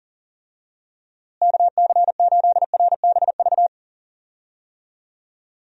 Morse code, heard as a keyed tone, about 40 wpm, 700 Hz. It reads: KC8RBV
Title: USA Call Signs - 6 Characters - Repeated 2x - 40wpm